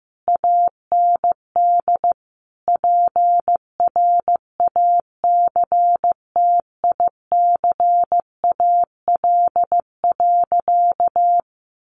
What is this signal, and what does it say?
Morse code, heard as a keyed tone, about 15 wpm, 700 Hz. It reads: AND PRACTICAL.
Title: Sentences from Top 1400 Words 15wpm